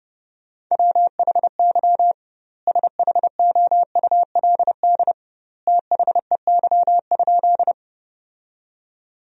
Morse code, heard as a keyed tone, about 32 wpm, 700 Hz. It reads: WHY SHOULD THEY?
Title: 2x Word-Spacing - Sentences from Top 300 Words - 30wpm